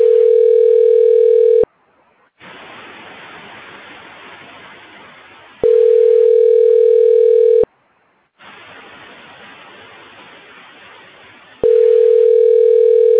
Ringback tone (North America), 440 + 480 Hz.